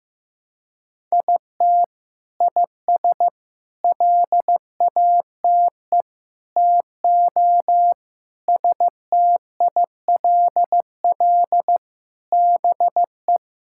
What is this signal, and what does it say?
Morse code, heard as a keyed tone, about 15 wpm, 700 Hz. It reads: IT IS LATE TO STILL BE
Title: Sentences from Top 1000 Words - Review Part 5 - 15wpm